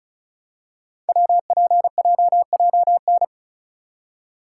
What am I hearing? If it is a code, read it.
Morse code, heard as a keyed tone, about 36 wpm, 700 Hz. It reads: WPJJN
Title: Code Groups - 5 Letters - 35wpm